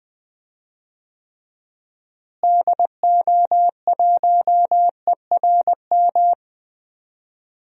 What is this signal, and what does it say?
Morse code, heard as a keyed tone, about 20 wpm, 700 Hz. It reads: DO1ERM
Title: DXCC-100 Call Signs - 6 Characters - Repeated 3x - 20wpm